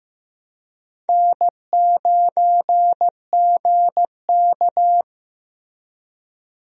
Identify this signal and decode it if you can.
Morse code, heard as a keyed tone, about 15 wpm, 700 Hz. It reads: N9GK
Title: USA Call Signs - 4 Characters - Repeated 3x - 15wpm